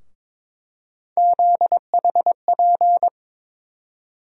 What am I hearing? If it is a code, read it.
Morse code, heard as a keyed tone, about 22 wpm, 700 Hz. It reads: ZHP
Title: USA Call Sign Postfixes - 3 Letters 22wpm